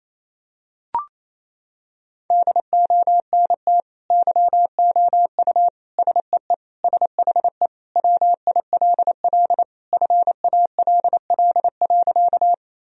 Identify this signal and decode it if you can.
Morse code, heard as a keyed tone, about 28 wpm, 700 Hz. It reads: DONT YOU SEE SHE WILL FALL.
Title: Sentences from Top 1000 Words - Review Part 2 - 28wpm